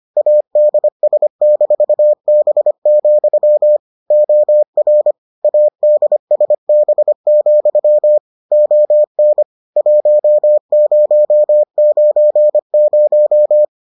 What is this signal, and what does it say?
Morse code, heard as a keyed tone, about 25 wpm, 600 Hz. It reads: ADS-B, OR ADSB, ON 1090